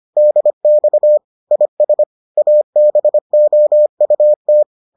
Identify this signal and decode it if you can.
Morse code, heard as a keyed tone, about 25 wpm, 600 Hz. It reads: DX IS ABOUT